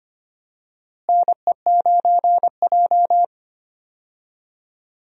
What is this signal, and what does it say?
Morse code, heard as a keyed tone, about 25 wpm, 700 Hz. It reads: NE9J
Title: USA Call Signs - 4 Characters - Repeated 3x - 25wpm